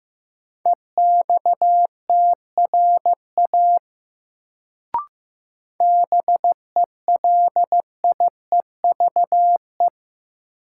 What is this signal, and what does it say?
Morse code, heard as a keyed tone, about 15 wpm, 700 Hz. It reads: EXTRA BELIEVE
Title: Sentences from Top 1000 Words - New Word Drill - 15wpm